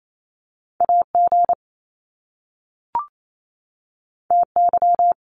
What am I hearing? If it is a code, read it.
Morse code, heard as a keyed tone, about 28 wpm, 700 Hz. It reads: AG TY
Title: Top Two Letter Combinations in Words 28wpm